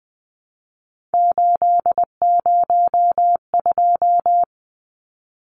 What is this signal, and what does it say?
Morse code, heard as a keyed tone, about 20 wpm, 700 Hz. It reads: 802
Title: Numbers - 3-Digit 20wpm